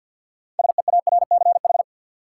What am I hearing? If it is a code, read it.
Morse code, heard as a keyed tone, about 53 wpm, 700 Hz. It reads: SEARCH
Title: Sentences from Top 800 Words - New Word Drill - 50wpm